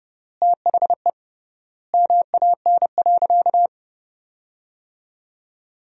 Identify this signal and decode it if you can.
Morse code, heard as a keyed tone, about 30 wpm, 700 Hz. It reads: THE MAN.
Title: 3x Word-Spacing - Sentences from Top 200 Words - 30wpm